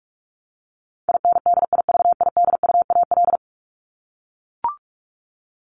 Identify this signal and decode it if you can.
Morse code, heard as a keyed tone, about 44 wpm, 700 Hz. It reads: INDIVIDUAL
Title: ES:S Suffix - Top 100 Words with Base Words 45wpm